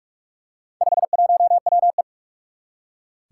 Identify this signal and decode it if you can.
Morse code, heard as a keyed tone, about 46 wpm, 700 Hz. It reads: 51WE
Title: Four Challenging Letter-Number Combinations 45wpm